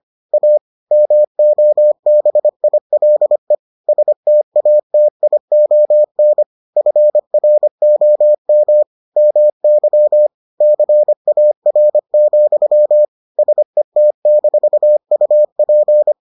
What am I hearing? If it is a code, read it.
Morse code, heard as a keyed tone, about 25 wpm, 600 Hz. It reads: A MOBILE STATION FROM MY CAR, SET-UP